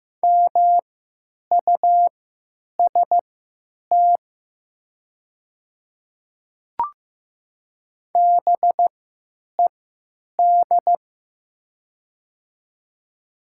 Morse code, heard as a keyed tone, about 15 wpm, spaced as for 9 wpm, 700 Hz. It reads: MUST BED
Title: 3x Character Spacing - Top 500 Words - 15wpm